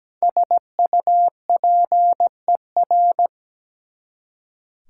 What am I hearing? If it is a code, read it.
Morse code, heard as a keyed tone, about 17 wpm, 700 Hz. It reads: SUPER
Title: Prefixes and Suffixes 17wpm